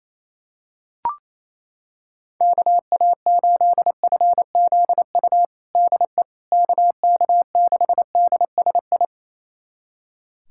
Morse code, heard as a keyed tone, about 28 wpm, 700 Hz. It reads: KA8FZU DE KK6DSI
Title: USA Callsign DE Callsign - 6 Characters - 28wpm